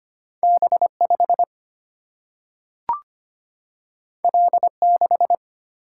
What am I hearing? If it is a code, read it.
Morse code, heard as a keyed tone, about 25 wpm, 700 Hz. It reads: B 5 L 6
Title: Single-Letter Single-Number 25wpm